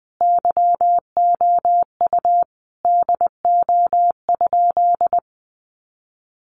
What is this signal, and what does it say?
Morse code, heard as a keyed tone, about 20 wpm, 700 Hz. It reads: YOU DO?
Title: Sentences from Top 100 Words 20wpm